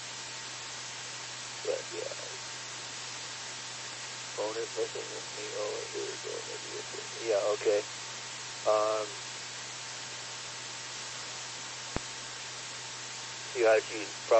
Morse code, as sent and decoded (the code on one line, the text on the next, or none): none